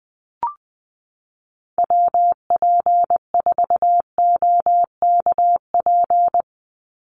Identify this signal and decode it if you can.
Morse code, heard as a keyed tone, about 20 wpm, 700 Hz. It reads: WP4OKP
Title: USA Call Signs 20wpm